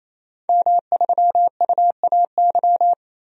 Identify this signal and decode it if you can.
Morse code, heard as a keyed tone, about 28 wpm, 700 Hz. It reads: M3UAY
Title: DXCC-100 Call Signs - 5 Characters - 28wpm